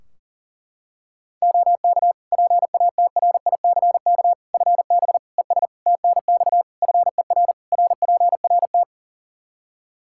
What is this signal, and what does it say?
Morse code, heard as a keyed tone, about 40 wpm, 700 Hz. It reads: OK PATRICK FB ES TNX FER RPRT
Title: Practice QSO - Exchange 2 - Info - 40wpm